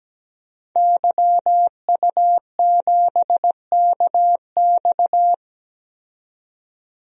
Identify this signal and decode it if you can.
Morse code, heard as a keyed tone, about 17 wpm, 700 Hz. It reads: YU7KX
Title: DXCC-100 Call Signs - 5 Characters - 17wpm